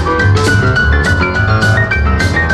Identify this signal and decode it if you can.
There is no code.